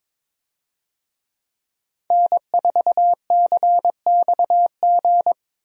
Morse code, heard as a keyed tone, about 22 wpm, 700 Hz. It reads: N4CXG
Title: USA Call Signs - 5 Characters - Repeated 3x - 22wpm